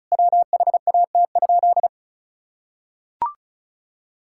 Morse code, heard as a keyed tone, about 36 wpm, 700 Hz. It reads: WHAT?
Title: Sentences from Top 1000 Words - Review Part 6 - 35wpm